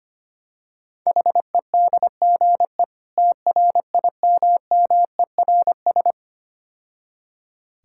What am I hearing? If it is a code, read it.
Morse code, heard as a keyed tone, about 27 wpm, 700 Hz. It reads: HEDGE TRIMMERS
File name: practice / 25wpm / Difficult Words - Speed-Racing with Hints - 25wpm